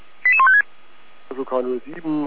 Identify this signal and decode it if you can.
Five-tone selective call (ZVEI), heard as quick runs of five tones, tones of about 70 ms, with speech. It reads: 88156
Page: ZVEI